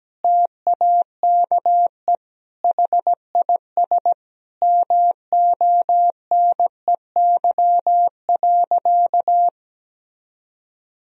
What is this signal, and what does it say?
Morse code, heard as a keyed tone, about 17 wpm, 700 Hz. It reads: TAKE HIS MONEY.
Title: Sentences from Top 1000 Words - Review Part 4 - 17wpm